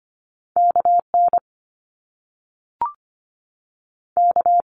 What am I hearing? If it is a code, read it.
Morse code, heard as a keyed tone, about 25 wpm, 700 Hz. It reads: KN K
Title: Top Two Letter Combinations in Words 25wpm